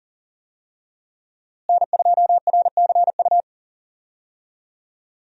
Morse code, heard as a keyed tone, about 40 wpm, 700 Hz. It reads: N2RCU
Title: USA Call Signs - 5 Characters - Repeated 2x - 40wpm